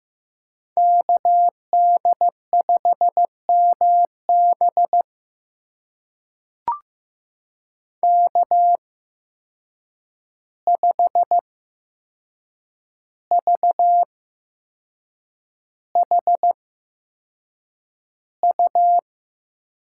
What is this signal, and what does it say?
Morse code, heard as a keyed tone, about 15 wpm, spaced as for 6 wpm, 700 Hz. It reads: KD5MB K5VHU
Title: USA Call Signs - 5 Characters - 8x Character Spacing - 15wpm